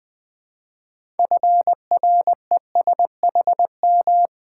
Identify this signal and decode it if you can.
Morse code, heard as a keyed tone, about 20 wpm, 690 Hz. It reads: FRESHM
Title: Build A Word - 3 Letters At A Time - 8-Letter Words - 20wpm